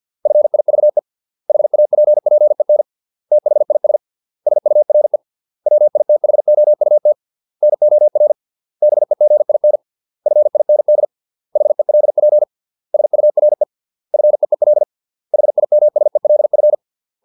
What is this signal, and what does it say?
Morse code, heard as a keyed tone, about 50 wpm, 600 Hz. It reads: FIVE HAPPEN THIS SURE WITHOUT NOR BEGIN FIND HELP SURE FEEL HIMSELF